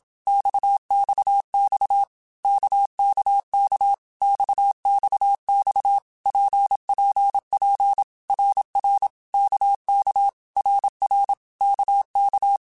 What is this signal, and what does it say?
Morse code, heard as a keyed tone, about 27 wpm, 800 Hz. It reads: XXX KKK XXX PPP RR KK RR KK
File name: Basic Letter Groups By7Groups / RKPX / RKPX@25